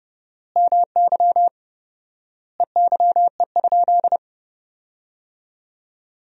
Morse code, heard as a keyed tone, about 30 wpm, 700 Hz. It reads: MY EYE?
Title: 4x Word-Spacing - Sentences from Top 500 Words - 30wpm